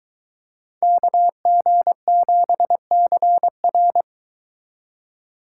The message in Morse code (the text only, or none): KG7CR